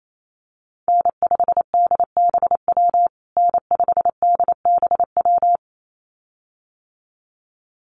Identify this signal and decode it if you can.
Morse code, heard as a keyed tone, about 28 wpm, 700 Hz. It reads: N5DBW N5DBW
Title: USA Call Signs Repeated 2x 28wpm